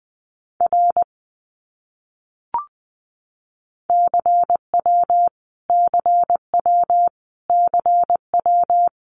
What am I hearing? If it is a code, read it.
Morse code, heard as a keyed tone, about 20 wpm, 700 Hz. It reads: R CW CW CW
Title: Abbreviations Repeated 3x - 20wpm